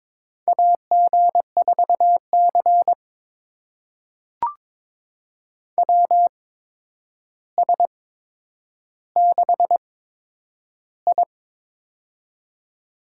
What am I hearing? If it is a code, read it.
Morse code, heard as a keyed tone, about 22 wpm, spaced as for 10 wpm, 700 Hz. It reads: AG4C WS6I
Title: USA Call Signs - 4 Characters - 8x Character Spacing - 22wpm